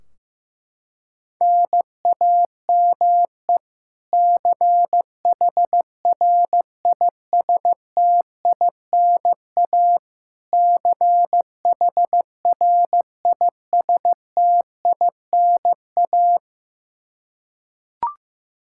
Morse code, heard as a keyed tone, about 15 wpm, 700 Hz. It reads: NAME CHRISTINA CHRISTINA
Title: K1USN SST QSO Practice 15wpm